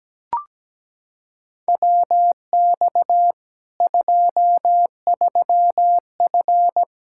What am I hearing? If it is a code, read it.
Morse code, heard as a keyed tone, about 17 wpm, 700 Hz. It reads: WX 23F
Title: K1USN SST QSO Practice 17wpm